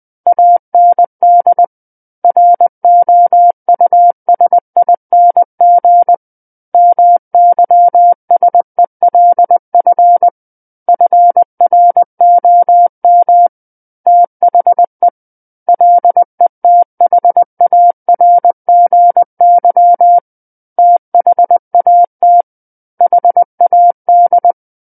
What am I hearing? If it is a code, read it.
Morse code, heard as a keyed tone, about 20 wpm, 700 Hz. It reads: AND ROUSING MYSELF FROM THE LETHARGY THAT HAD
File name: 20 WPM